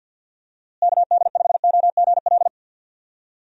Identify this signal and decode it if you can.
Morse code, heard as a keyed tone, about 51 wpm, 700 Hz. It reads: KD5OGL
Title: USA Call Signs - 6 Characters - Repeated 3x - 50wpm